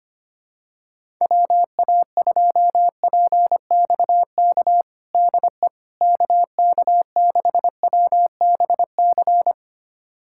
Morse code, heard as a keyed tone, about 25 wpm, 700 Hz. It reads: WA2PXK DE KK6WBC
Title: USA Callsign DE Callsign - 6 Characters - 25wpm